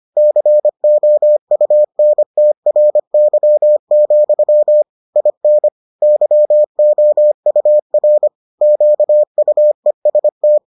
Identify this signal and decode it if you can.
Morse code, heard as a keyed tone, about 25 wpm, 600 Hz. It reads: COUNTRY, IN YOUR QUEST